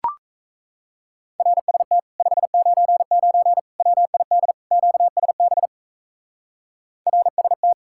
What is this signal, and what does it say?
Morse code, heard as a keyed tone, about 36 wpm, 700 Hz. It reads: RST 599 WID QSB RST